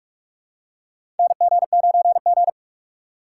Morse code, heard as a keyed tone, about 46 wpm, 700 Hz. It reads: NG9G
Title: USA Call Signs - 2x Character Spacing - 45wpm